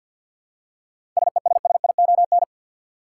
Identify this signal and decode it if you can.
Morse code, heard as a keyed tone, about 51 wpm, 700 Hz. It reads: SESSION
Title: Seven Letter Words - 50wpm